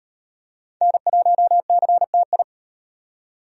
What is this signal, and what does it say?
Morse code, heard as a keyed tone, about 38 wpm, 700 Hz. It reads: N1CTI